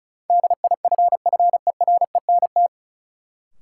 Morse code, heard as a keyed tone, about 36 wpm, 700 Hz. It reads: DIFFERENT
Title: Sentences from Top 600 Words - New Word Drill - 35wpm